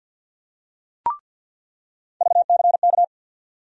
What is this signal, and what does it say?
Morse code, heard as a keyed tone, about 48 wpm, 700 Hz. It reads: VCK
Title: USA Call Sign Postfixes - 3 Letters 50wpm